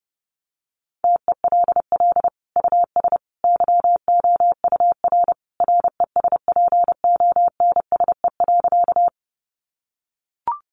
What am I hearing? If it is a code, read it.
Morse code, heard as a keyed tone, about 28 wpm, 700 Hz. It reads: TELL US YOUR RESPONSE.
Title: Sentences from Top 1400 Words 30wpm